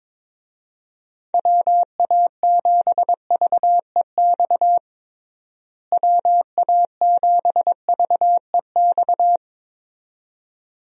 Morse code, heard as a keyed tone, about 21 wpm, 700 Hz. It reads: WA7VEX WA7VEX
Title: USA Call Signs - 6 Characters - Repeated 2x - 22wpm